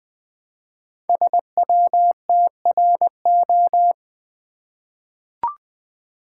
Morse code, heard as a keyed tone, about 20 wpm, 700 Hz. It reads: SWTRO
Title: Code Groups - 5 Letters - 20wpm